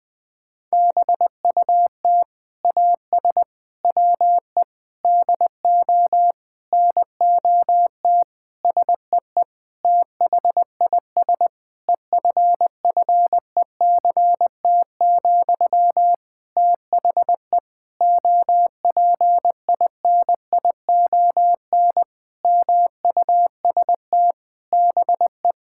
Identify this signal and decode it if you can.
Morse code, heard as a keyed tone, about 20 wpm, 700 Hz. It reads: BUT AS WE DO NOT SEE THIS EFFECT, THE OPINION MUST BE
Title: Sentences from Top 1600 Words 20wpm